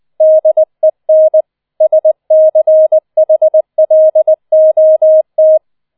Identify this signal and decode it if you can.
Morse code, heard as a keyed tone, about 20 wpm, 630 Hz. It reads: DEN SCHLOT